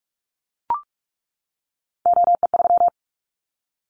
Morse code, heard as a keyed tone, about 44 wpm, 700 Hz. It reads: OE3
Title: Callsign Prefixes for Top 100 DXCC - 45wpm